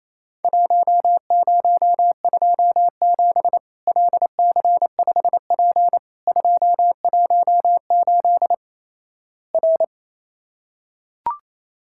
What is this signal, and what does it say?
Morse code, heard as a keyed tone, about 28 wpm, 700 Hz. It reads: 1027 LC5P 218 R
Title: Worked All Europe DX Contest - QTC Exchange - 28wpm